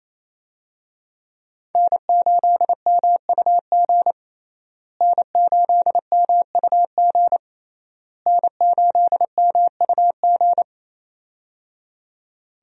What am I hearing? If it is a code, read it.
Morse code, heard as a keyed tone, about 27 wpm, 700 Hz. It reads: N8MUG N8MUG N8MUG
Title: USA Call Signs - 5 Characters - Repeated 3x - 28wpm